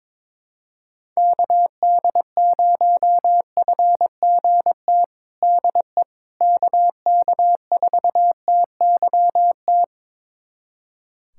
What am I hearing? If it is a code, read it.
Morse code, heard as a keyed tone, about 21 wpm, 700 Hz. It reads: KD0FGT DE KK4TYT
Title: USA Callsign DE Callsign - 6 Characters - 22wpm